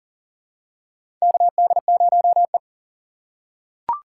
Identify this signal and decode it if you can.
Morse code, heard as a keyed tone, about 40 wpm, 700 Hz. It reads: KD0E
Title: USA Call Signs - 4x Character Spacing - 40wpm